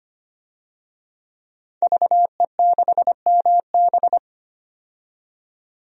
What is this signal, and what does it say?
Morse code, heard as a keyed tone, about 25 wpm, 700 Hz. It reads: VE6MB
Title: DXCC-100 Call Signs - 5 Characters - 25wpm